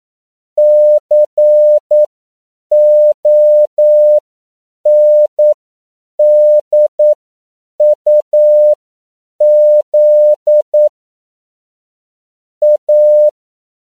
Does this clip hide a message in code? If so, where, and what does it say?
Yes: Morse code from 0.6 to 13.3 s, heard as a keyed tone, about 9 wpm, spaced as for 7 wpm, 600 Hz. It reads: CONDUZ A